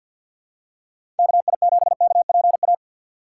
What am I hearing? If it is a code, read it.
Morse code, heard as a keyed tone, about 49 wpm, 700 Hz. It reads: KI7KPA
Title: USA Call Signs - 6 Characters - 4x Character Spacing - 50wpm